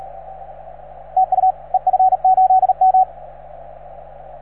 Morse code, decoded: TU EF8M